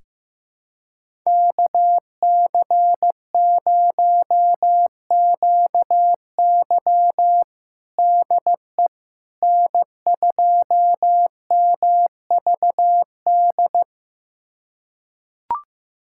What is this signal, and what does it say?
Morse code, heard as a keyed tone, about 15 wpm, 700 Hz. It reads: KC0QY DE N2MVD
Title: USA Callsign DE Callsign - 5 Characters - 15wpm